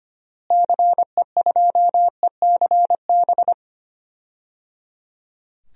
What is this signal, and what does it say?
Morse code, heard as a keyed tone, about 25 wpm, 700 Hz. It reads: CE2ECB
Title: DXCC-100 Call Signs - 6 Characters - Repeated 3x - 25wpm